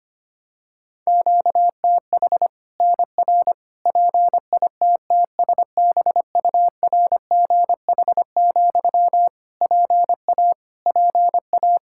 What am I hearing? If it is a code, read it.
Morse code, heard as a keyed tone, about 25 wpm, 700 Hz. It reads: QTH NR PITTSBURGH, PA PA
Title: Practice QSO - Exchange 1 - Beginning - 25wpm